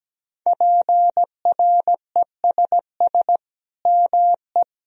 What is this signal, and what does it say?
Morse code, heard as a keyed tone, about 17 wpm, 700 Hz. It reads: PRESS ME
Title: Sentences from Top 1000 Words - Review Part 1 - 17wpm